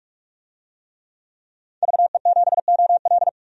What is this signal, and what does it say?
Morse code, heard as a keyed tone, about 44 wpm, 690 Hz. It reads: VE7OL
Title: DXCC-100 Call Signs - 5 Characters - 45wpm